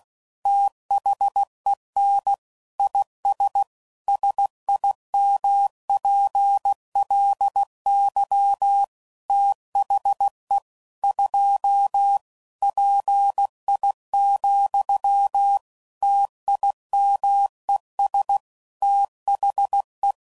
Morse code, heard as a keyed tone, about 16 wpm, 800 Hz. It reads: THEN IS SIMPLY THE 2 PI, TIMES THE
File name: CW Code Groups @15WPM / Technical Subjects / XC